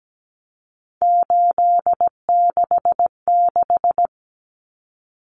Morse code, heard as a keyed tone, about 17 wpm, 700 Hz. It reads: 866
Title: Common Cut Numbers - 3-Digit 17wpm